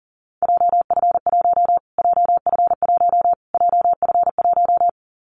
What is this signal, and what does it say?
Morse code, heard as a keyed tone, about 40 wpm, 700 Hz. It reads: JF1 JF1 JF1
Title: Callsign Prefixes for Top 25 DXCC - Repeated 3x - 40wpm